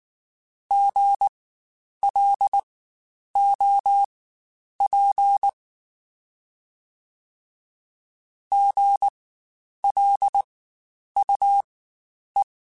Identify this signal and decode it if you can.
Morse code, heard as a keyed tone, about 19 wpm, spaced as for 9 wpm, 800 Hz. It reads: GLOP GLUE
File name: Spaced Alphas / Spaced Alphas mp3s / Gs sp @18